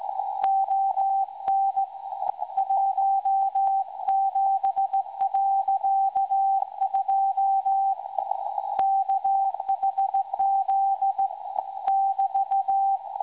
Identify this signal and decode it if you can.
Morse code, heard as a keyed tone, about 17 wpm, 780 Hz. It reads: ON 17.2 KHZ <BT>
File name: WebSDR records / SAQ recording 17.2kHz